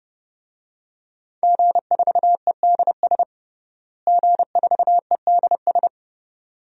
Morse code, heard as a keyed tone, about 28 wpm, 700 Hz. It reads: G4EDS G4EDS
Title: DXCC-100 Call Signs - 5 Characters - Repeated 2x - 30wpm